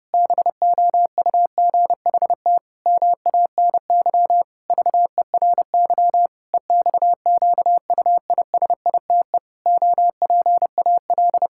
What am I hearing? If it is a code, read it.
Morse code, heard as a keyed tone, about 32 wpm, 700 Hz. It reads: BOUGHT MANY VERY EXQUISITE OPAL
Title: Pangram Challenge v2 - 30wpm